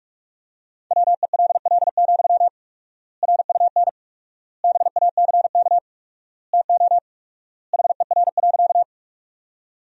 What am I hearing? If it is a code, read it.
Morse code, heard as a keyed tone, about 46 wpm, 700 Hz. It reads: WELL, RUN BACK TO HER.
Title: 4x Word-Spacing - Sentences from Top 300 Words - 45wpm